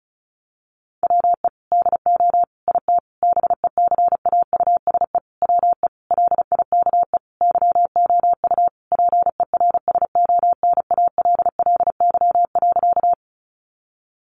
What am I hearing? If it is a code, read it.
Morse code, heard as a keyed tone, about 34 wpm, 700 Hz. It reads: WE DO IT BECAUSE WE LIKE YOU PERSONALLY.
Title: Sentences from Top 1400 Words 35wpm